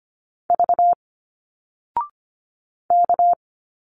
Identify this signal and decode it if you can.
Morse code, heard as a keyed tone, about 25 wpm, 700 Hz. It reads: V K
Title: Single Letters 25wpm